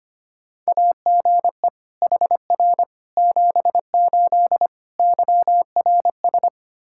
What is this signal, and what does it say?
Morse code, heard as a keyed tone, about 25 wpm, 700 Hz. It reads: AGE HR 78 YRS
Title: Practice QSO - Exchange 3 - More Info - 25wpm